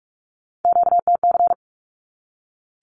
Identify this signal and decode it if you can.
Morse code, heard as a keyed tone, about 44 wpm, 700 Hz. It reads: QTC
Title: Q-Codes 45wpm